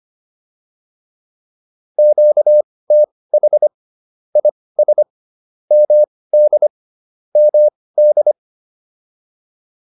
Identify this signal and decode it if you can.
Morse code, heard as a keyed tone, about 25 wpm, spaced as for 18 wpm, 600 Hz. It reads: QTH IS MD MD